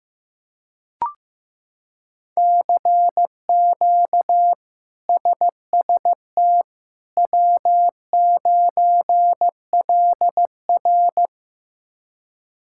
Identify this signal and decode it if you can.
Morse code, heard as a keyed tone, about 15 wpm, 700 Hz. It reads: CQ SST W9LR
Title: K1USN SST QSO Practice - 15wpm